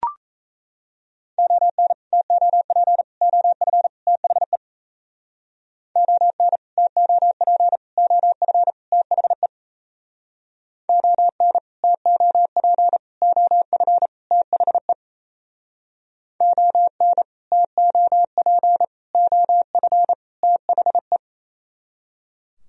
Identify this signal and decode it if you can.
Morse code, heard as a keyed tone, about 38 wpm, 700 Hz. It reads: ON TOP OF THE ON TOP OF THE ON TOP OF THE ON TOP OF THE